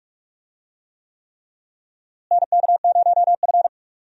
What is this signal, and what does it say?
Morse code, heard as a keyed tone, about 46 wpm, 700 Hz. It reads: NK0F
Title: USA Call Signs - 4 Characters - Repeated 3x - 45wpm